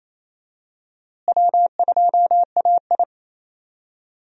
Morse code, heard as a keyed tone, about 28 wpm, 700 Hz. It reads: W2AI
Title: USA Call Signs - 4 Characters - Repeated 3x - 28wpm